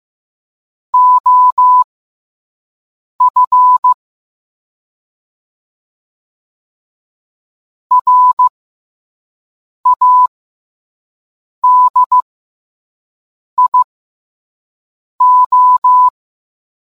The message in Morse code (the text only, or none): OF RADIO